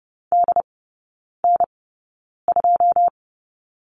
Morse code, heard as a keyed tone, about 30 wpm, spaced as for 9 wpm, 700 Hz. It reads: DN2
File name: 081 Lesson 14e - Number 2 - Three Characters 30wpm